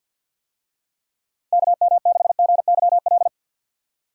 Morse code, heard as a keyed tone, about 48 wpm, 700 Hz. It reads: KM6GYL
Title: USA Call Signs - 6 Characters - 50wpm